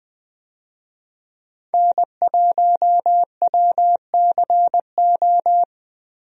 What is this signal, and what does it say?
Morse code, heard as a keyed tone, about 20 wpm, 700 Hz. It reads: N1WCO
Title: USA Call Signs - 5 Characters - Repeated 3x - 20wpm